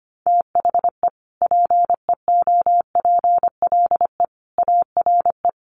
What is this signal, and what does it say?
Morse code, heard as a keyed tone, about 25 wpm, 700 Hz. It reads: THE PEOPLE ARE
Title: Sentences from Top 1300 Words 25wpm